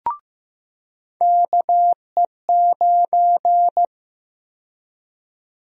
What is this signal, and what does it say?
Morse code, heard as a keyed tone, about 15 wpm, 700 Hz. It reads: KE9
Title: USA Call Sign Prefixes - 15wpm